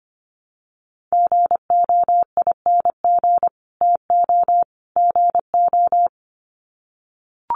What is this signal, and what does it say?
Morse code, heard as a keyed tone, about 25 wpm, 700 Hz. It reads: GOING TO GO
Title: Sets of 3 Words 25wpm